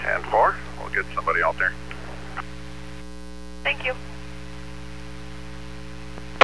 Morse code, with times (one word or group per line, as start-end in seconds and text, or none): none